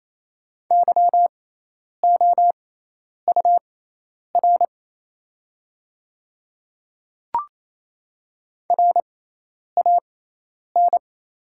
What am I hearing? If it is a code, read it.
Morse code, heard as a keyed tone, about 28 wpm, spaced as for 10 wpm, 700 Hz. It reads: YOUR RAN
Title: 6x Character Spacing - Top 500 Words - 28wpm